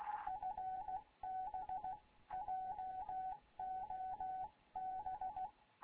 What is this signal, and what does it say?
Morse code, heard as a keyed tone, about 16 wpm, 750 Hz. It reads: FB JOB
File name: Morze-HA2MN / hangp / 2 11 GYAKORLAT